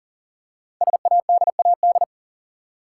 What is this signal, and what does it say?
Morse code, heard as a keyed tone, about 38 wpm, 690 Hz. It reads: SADAD
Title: Code Groups - 5 Letters - 40wpm